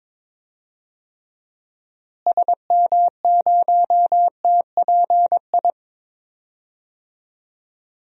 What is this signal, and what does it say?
Morse code, heard as a keyed tone, about 22 wpm, 700 Hz. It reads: SM0TPI
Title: DXCC-100 Call Signs - 6 Characters - Repeated 2x - 22wpm